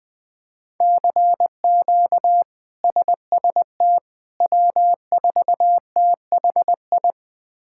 Morse code, heard as a keyed tone, about 19 wpm, 700 Hz. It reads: CQ SST W4THI
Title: K1USN SST QSO Practice - 20wpm